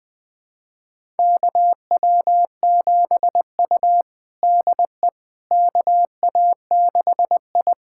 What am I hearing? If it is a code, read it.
Morse code, heard as a keyed tone, about 20 wpm, 700 Hz. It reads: KW7U DE KA6I
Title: USA Callsign DE Callsign - 4 Characters - 20wpm